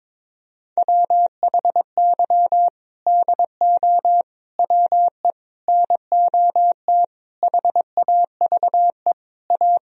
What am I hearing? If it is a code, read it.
Morse code, heard as a keyed tone, about 21 wpm, 700 Hz. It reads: WHY DO WE NOT HAVE A
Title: Sentences from Top 1000 Words - Review Part 3 - 22wpm